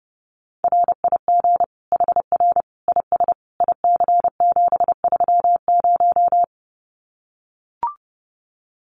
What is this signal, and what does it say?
Morse code, heard as a keyed tone, about 30 wpm, 700 Hz. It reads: RIG HR IS IC730
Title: Common QSO Phrases 30wpm